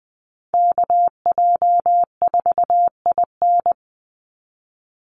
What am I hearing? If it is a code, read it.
Morse code, heard as a keyed tone, about 20 wpm, 700 Hz. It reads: KJ4IN
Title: USA Call Signs 20wpm